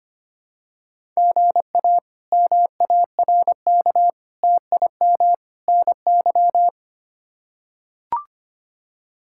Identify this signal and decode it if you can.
Morse code, heard as a keyed tone, about 25 wpm, 700 Hz. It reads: GA MARK TIM NY
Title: K1USN SST QSO Practice - 25wpm